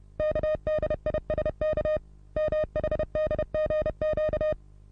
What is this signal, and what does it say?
Morse code, heard as a keyed tone, about 34 wpm, 620 Hz. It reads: KDISK MHDGQ
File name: repos / Morsekurs / DJ4UF Morsen27